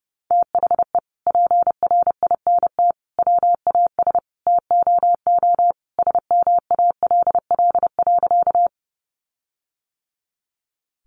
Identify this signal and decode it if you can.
Morse code, heard as a keyed tone, about 30 wpm, 700 Hz. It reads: THE PRINT WAS TOO SMALL.
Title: Sentences from Top 1400 Words 30wpm